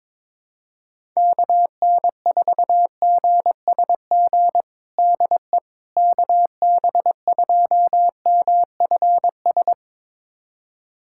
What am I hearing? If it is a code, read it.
Morse code, heard as a keyed tone, about 22 wpm, 700 Hz. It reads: KN4GSG DE KB2MFS
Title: USA Callsign DE Callsign - 6 Characters - 22wpm